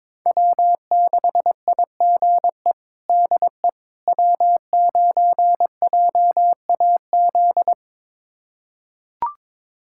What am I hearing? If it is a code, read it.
Morse code, heard as a keyed tone, about 22 wpm, 700 Hz. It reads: W6IGE DE W9JAZ
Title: USA Callsign DE Callsign - 5 Characters - 22wpm